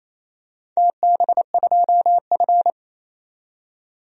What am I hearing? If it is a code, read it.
Morse code, heard as a keyed tone, about 27 wpm, 690 Hz. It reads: TB2F
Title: DXCC-100 Call Signs - 4 Characters - 28wpm